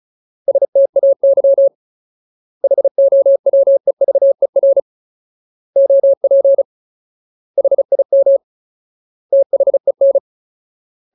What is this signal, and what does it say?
Morse code, heard as a keyed tone, about 34 wpm, 550 Hz. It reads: STAY HOWEVER OP HIM THEN
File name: Ultimate Word List / 35WPM Ultimate